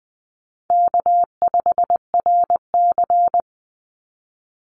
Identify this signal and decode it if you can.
Morse code, heard as a keyed tone, about 20 wpm, 700 Hz. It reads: K5RC
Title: Easy Call Signs Repeated 2x - 20wpm